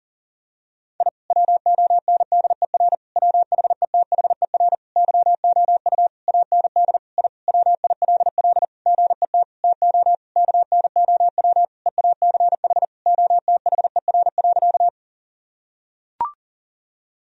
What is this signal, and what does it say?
Morse code, heard as a keyed tone, about 40 wpm, 700 Hz. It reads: I WONDER WHETHER YOU AND I WILL GET TO KNOW EACH OTHER.
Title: Sentences from Top 1000 Words - Review Part 3 - 40wpm